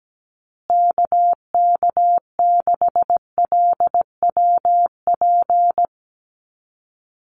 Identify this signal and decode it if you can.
Morse code, heard as a keyed tone, about 17 wpm, 700 Hz. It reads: KK6LWP
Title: USA Call Signs Repeated 2x 17wpm